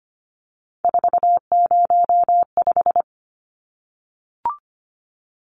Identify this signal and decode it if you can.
Morse code, heard as a keyed tone, about 25 wpm, 700 Hz. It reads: 405
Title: Numbers - 3-Digit 25wpm